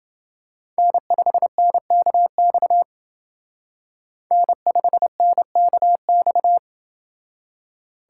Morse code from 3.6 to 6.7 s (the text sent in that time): N5NKX